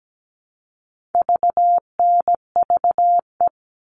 Morse code, heard as a keyed tone, about 17 wpm, 700 Hz. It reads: VNVE
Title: All Cut Numbers - 4-Digit 17wpm